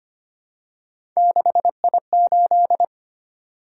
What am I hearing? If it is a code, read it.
Morse code, heard as a keyed tone, about 25 wpm, 700 Hz. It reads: 6I8
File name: Three Letter-Number Combinations - 25wpm